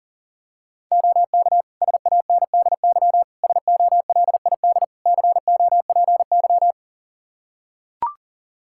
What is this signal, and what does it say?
Morse code, heard as a keyed tone, about 40 wpm, 700 Hz. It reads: OK SANDY SOLID COPY
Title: K1USN SST QSO Practice 40wpm